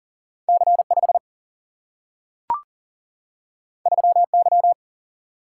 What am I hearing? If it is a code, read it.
Morse code, heard as a keyed tone, about 39 wpm, 700 Hz. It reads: C5 3Y
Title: Single-Letter Single-Number 40wpm